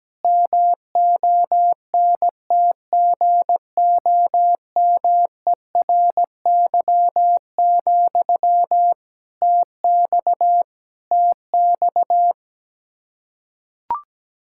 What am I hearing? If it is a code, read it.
Morse code, heard as a keyed tone, about 17 wpm, 700 Hz. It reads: MONTGOMERY, TX TX
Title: Practice QSO - Exchange 1 - Beginning - 17wpm